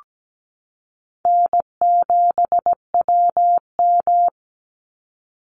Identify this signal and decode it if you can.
Morse code, heard as a keyed tone, about 17 wpm, 700 Hz. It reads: N7WM
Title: NA SOTA Chaser Callsigns 17wpm